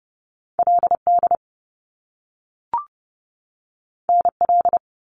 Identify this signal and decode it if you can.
Morse code, heard as a keyed tone, about 30 wpm, 700 Hz. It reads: LD NL
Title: Top Two Letter Combinations in Words 30wpm